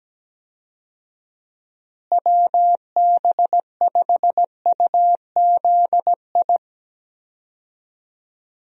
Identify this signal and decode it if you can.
Morse code, heard as a keyed tone, about 17 wpm, 700 Hz. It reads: WB5UZI
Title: USA Call Signs - 6 Characters - Repeated 3x - 17wpm